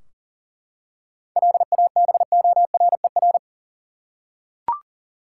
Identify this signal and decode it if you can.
Morse code, heard as a keyed tone, about 40 wpm, 700 Hz. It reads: LABORER
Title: Practice QSO - Exchange 3 - More Info - 40wpm